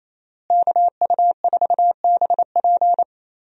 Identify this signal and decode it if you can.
Morse code, heard as a keyed tone, about 28 wpm, 700 Hz. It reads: KU4BP
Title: USA Call Signs - 5 Characters - Repeated 2x - 28wpm